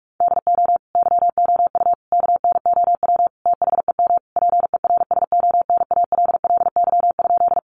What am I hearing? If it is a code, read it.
Morse code, heard as a keyed tone, about 44 wpm, 700 Hz. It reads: DO YOU KNOW THEM PERSONALLY?
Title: Sentences from Top 1400 Words 45wpm